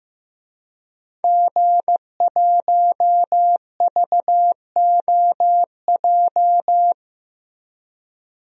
Morse code, heard as a keyed tone, about 15 wpm, 700 Hz. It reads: G1VOJ
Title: DXCC-100 Call Signs - 5 Characters - Repeated 3x - 15wpm